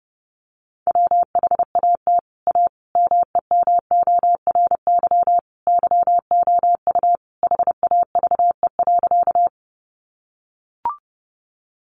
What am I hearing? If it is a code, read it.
Morse code, heard as a keyed tone, about 30 wpm, 700 Hz. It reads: WHAT A MEMORY YOU HAVE.
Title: Sentences from Top 1300 Words 30wpm